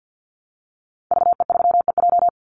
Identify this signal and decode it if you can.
Morse code, heard as a keyed tone, about 50 wpm, 700 Hz. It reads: VE3EJ
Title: Call Signs Repeated 2x - 50wpm